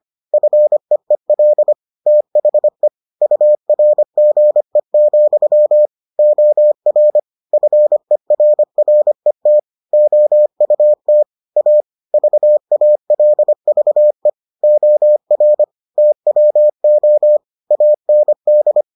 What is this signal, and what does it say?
Morse code, heard as a keyed tone, about 25 wpm, 600 Hz. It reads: FEEL THE URGE, OR FERRET OUT A VALVE OR TWO AND